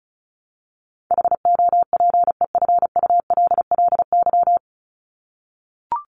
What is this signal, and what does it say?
Morse code, heard as a keyed tone, about 34 wpm, 700 Hz. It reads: HOPEFULLY
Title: Top 1251-1500 Words 35wpm